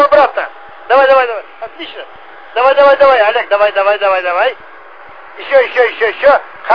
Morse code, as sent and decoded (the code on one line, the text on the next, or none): none